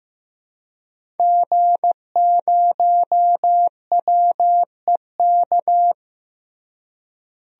Morse code, heard as a keyed tone, about 16 wpm, 700 Hz. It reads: G0WEK
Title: DXCC-100 Call Signs - 5 Characters - Repeated 2x - 15wpm